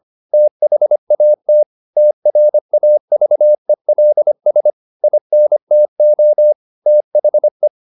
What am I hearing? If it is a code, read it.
Morse code, heard as a keyed tone, about 25 wpm, 600 Hz. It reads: THAT TRAVELS INTO THE